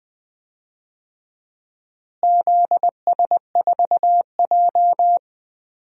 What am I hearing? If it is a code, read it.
Morse code, heard as a keyed tone, about 20 wpm, 700 Hz. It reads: ZS4J
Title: DXCC-100 Call Signs - 4 Characters - Repeated 3x - 20wpm